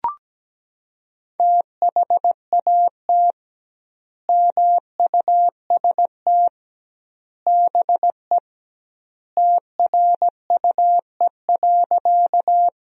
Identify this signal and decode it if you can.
Morse code, heard as a keyed tone, about 17 wpm, 700 Hz. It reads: THAT MUST BE TRUE.
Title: 2x Word-Spacing - Sentences from Top 400 Words - 17wpm